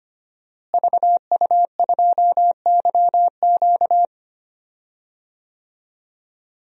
Morse code, heard as a keyed tone, about 25 wpm, 700 Hz. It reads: VU2YQ
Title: DXCC-100 Call Signs - 5 Characters - Repeated 3x - 25wpm